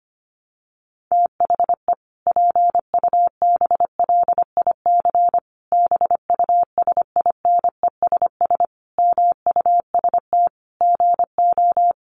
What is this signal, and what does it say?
Morse code, heard as a keyed tone, about 25 wpm, 700 Hz. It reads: THE PUBLIC BUSINESS MUST GO
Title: Sentences from Top 1300 Words 25wpm